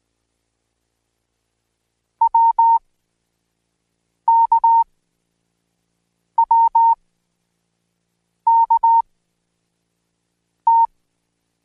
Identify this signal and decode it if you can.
Morse code, heard as a keyed tone, about 18 wpm, 920 Hz. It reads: W K W K T
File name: Code Quick / AudioDownld1 / Seg10-Day1-DayDrill 1